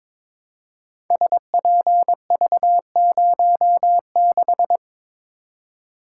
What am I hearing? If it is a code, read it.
Morse code, heard as a keyed tone, about 22 wpm, 700 Hz. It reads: SPV06
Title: Code Groups - 5 Letter:Numbers - 22wpm